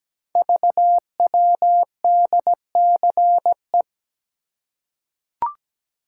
Morse code, heard as a keyed tone, about 17 wpm, 700 Hz. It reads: VWDCE
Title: Code Groups - 5 Letters - 17wpm